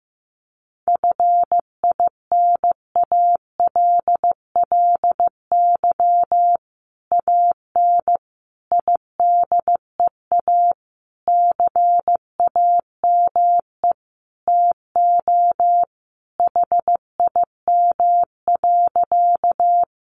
Morse code, heard as a keyed tone, about 15 wpm, 700 Hz. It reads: FINALLY AN IDEA CAME TO HIM.
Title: Sentences from Top 1000 Words 15wpm